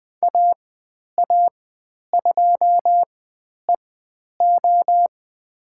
Morse code, heard as a keyed tone, about 20 wpm, spaced as for 10 wpm, 700 Hz. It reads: AA2EO
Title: 102 Lesson 17g - Letter u - Call Signs 20wpm